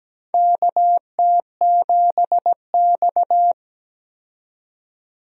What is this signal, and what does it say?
Morse code, heard as a keyed tone, about 17 wpm, 700 Hz. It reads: KT7X
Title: USA Call Signs - 4 Characters - 17wpm